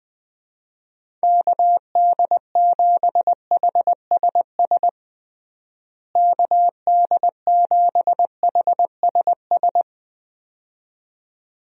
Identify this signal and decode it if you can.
Morse code, heard as a keyed tone, about 20 wpm, 700 Hz. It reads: KD7HSS KD7HSS
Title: USA Call Signs - 6 Characters - Repeated 2x - 20wpm